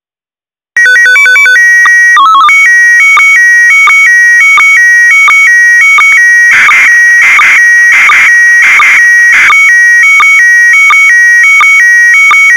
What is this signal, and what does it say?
SSTV (slow-scan television), PD 90.
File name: SSTV OVER FM IMAGE TWO 146805 UNDR NET